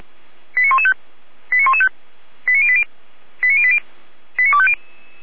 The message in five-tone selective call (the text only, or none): 89115 89115 89080 89080 89250